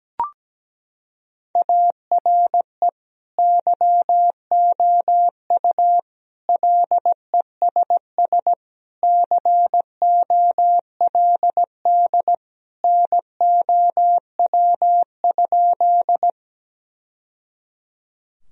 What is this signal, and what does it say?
Morse code, heard as a keyed tone, about 17 wpm, 700 Hz. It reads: ARE YOU LESS COLD NOW?
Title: Sentences from Top 1000 Words - Review Part 2 - 17wpm